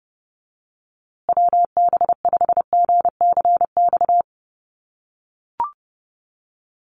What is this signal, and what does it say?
Morse code, heard as a keyed tone, about 30 wpm, 700 Hz. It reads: WB5GCX
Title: USA Call Signs Repeated 2x 30wpm